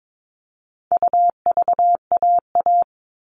Morse code, heard as a keyed tone, about 22 wpm, 700 Hz. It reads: UVAA
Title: All Cut Numbers - 4-Digit 22wpm